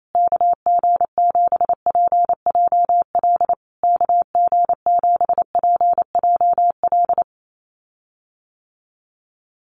Morse code, heard as a keyed tone, about 28 wpm, 700 Hz. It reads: KG7PJL KG7PJL
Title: USA Call Signs Repeated 3x 28wpm